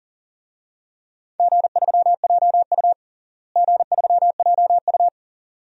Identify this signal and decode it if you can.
Morse code, heard as a keyed tone, about 40 wpm, 700 Hz. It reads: G3JU G3JU
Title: DXCC-100 Call Signs - 4 Characters - Repeated 2x - 40wpm